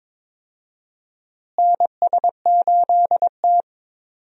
Morse code, heard as a keyed tone, about 22 wpm, 700 Hz. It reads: NS8T
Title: USA Call Signs - 4 Characters - Repeated 2x - 22wpm